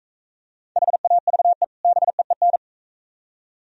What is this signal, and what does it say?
Morse code, heard as a keyed tone, about 43 wpm, 700 Hz. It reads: HAVE BEEN